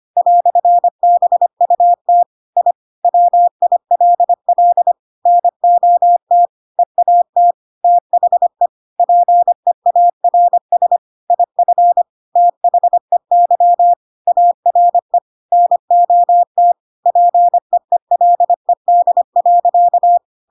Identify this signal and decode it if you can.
Morse code, heard as a keyed tone, about 25 wpm, 700 Hz. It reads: "BUT I WILL NOT EAT THE PEARS IF THEY ARE NOT PEELED.
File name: Pinocchio / Pinocchio ch7 25 25 ik1hge